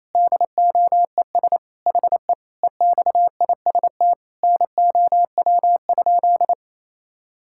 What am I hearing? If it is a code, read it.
Morse code, heard as a keyed tone, about 28 wpm, 700 Hz. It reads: DOES HE EXIST NOW?
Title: Sentences from Top 1000 Words - Review Part 5 - 28wpm